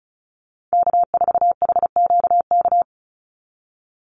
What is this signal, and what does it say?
Morse code, heard as a keyed tone, about 34 wpm, 700 Hz. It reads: K4HQK
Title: Call Signs Repeated 2x - 35wpm